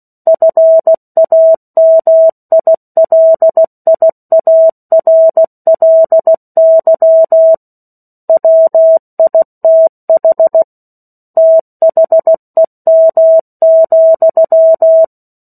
Morse code, heard as a keyed tone, about 16 wpm, 650 Hz. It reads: FAMILIARLY WITH THEM,
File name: 03-Chapter-03 16 WPM